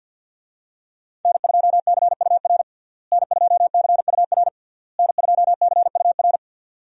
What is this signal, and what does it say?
Morse code, heard as a keyed tone, about 52 wpm, 700 Hz. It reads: N2CUR N2CUR N2CUR
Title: USA Call Signs - 5 Characters - Repeated 3x - 50wpm